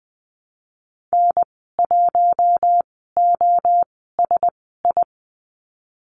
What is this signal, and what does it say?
Morse code, heard as a keyed tone, about 20 wpm, spaced as for 15 wpm, 700 Hz. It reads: N1OSI
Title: 048 Lesson 9g - Number 4 - Call Signs 20wpm